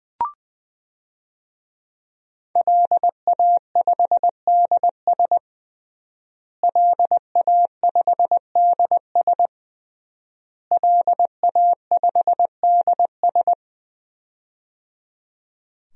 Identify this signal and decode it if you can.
Morse code, heard as a keyed tone, about 20 wpm, 700 Hz. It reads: LA5DS LA5DS LA5DS
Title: DXCC-100 Call Signs - 5 Characters - Repeated 3x - 20wpm